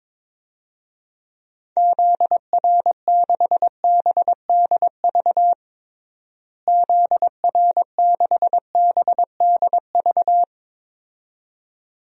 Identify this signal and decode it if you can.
Morse code, heard as a keyed tone, about 22 wpm, 700 Hz. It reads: ZR6BDV ZR6BDV
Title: DXCC-100 Call Signs - 6 Characters - Repeated 2x - 22wpm